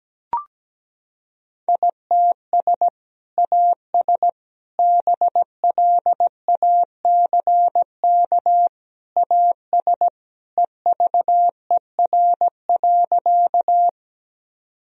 Morse code, heard as a keyed tone, about 17 wpm, 700 Hz. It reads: ITS AS BLACK AS EVER.
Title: Sentences from Top 1000 Words - Review Part 2 - 17wpm